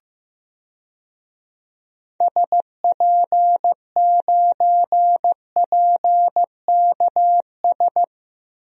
Morse code, heard as a keyed tone, about 15 wpm, 700 Hz. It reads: SP9PKS
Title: DXCC-100 Call Signs - 6 Characters - Repeated 2x - 15wpm